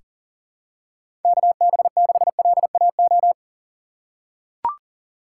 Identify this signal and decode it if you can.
Morse code, heard as a keyed tone, about 37 wpm, 700 Hz. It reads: KB6LAO
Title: USA Call Signs - 6 Characters - 4x Character Spacing - 40wpm